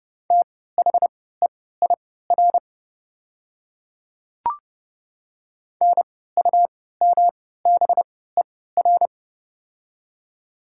Morse code, heard as a keyed tone, about 30 wpm, spaced as for 17 wpm, 700 Hz. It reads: THEIR NUMBER
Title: 3x Character Spacing - Top 200 Words - 30wpm